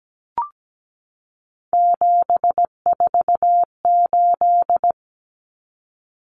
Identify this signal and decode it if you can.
Morse code, heard as a keyed tone, about 17 wpm, 690 Hz. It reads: 748
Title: Common Cut Numbers - 3-Digit 17wpm